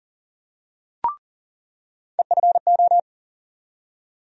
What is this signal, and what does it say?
Morse code, heard as a keyed tone, about 40 wpm, 700 Hz. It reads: EFO
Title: USA Call Sign Postfixes - 3 Letters 40wpm